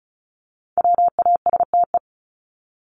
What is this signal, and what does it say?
Morse code, heard as a keyed tone, about 34 wpm, 700 Hz. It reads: WASTE
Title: Top 1751-2000 Words 35wpm